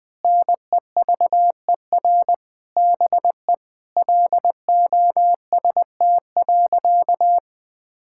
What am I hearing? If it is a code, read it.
Morse code, heard as a keyed tone, about 19 wpm, 700 Hz. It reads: NEVER BE LOST.
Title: Sentences from Top 1600 Words 20wpm